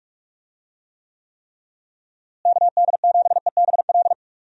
Morse code, heard as a keyed tone, about 46 wpm, 700 Hz. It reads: KD7EBL
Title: USA Call Signs - 6 Characters - Repeated 3x - 45wpm